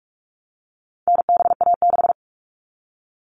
Morse code, heard as a keyed tone, about 44 wpm, 700 Hz. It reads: NBA6
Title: All Cut Numbers - 4-Digit 45wpm